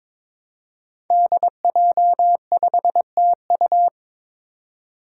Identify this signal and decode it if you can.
Morse code, heard as a keyed tone, about 22 wpm, 700 Hz. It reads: DJ5TU
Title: Five Challenging Letter-Number Combinations 22wpm